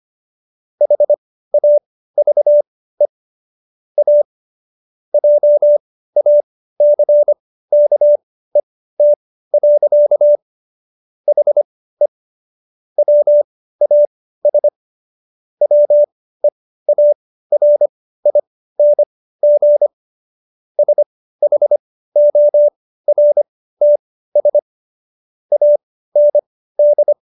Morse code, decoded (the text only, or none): HAVE A JACKET. HE WAS WEARING SHORTS AND